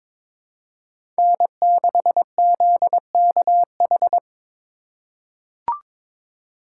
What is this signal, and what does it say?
Morse code, heard as a keyed tone, about 23 wpm, 700 Hz. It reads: N6ZKH
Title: USA Call Signs - 3x Character Spacing - 22wpm